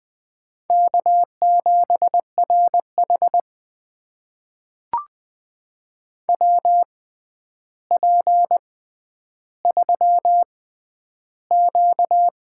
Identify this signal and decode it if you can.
Morse code, heard as a keyed tone, about 20 wpm, spaced as for 10 wpm, 700 Hz. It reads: K7RH WP3Q
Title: USA Call Signs - 4 Characters - 6x Character Spacing - 20wpm